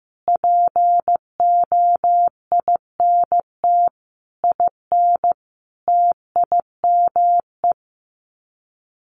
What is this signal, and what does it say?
Morse code, heard as a keyed tone, about 15 wpm, 700 Hz. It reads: POINT IN TIME
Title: Sets of 5 Words Encore 15wpm